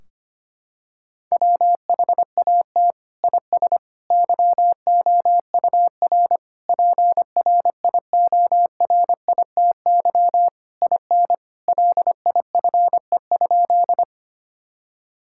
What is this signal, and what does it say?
Morse code, heard as a keyed tone, about 25 wpm, 700 Hz. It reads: WHAT IS YOUR PRIORITY IN LIFE?
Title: Sentences from Top 1600 Words 25wpm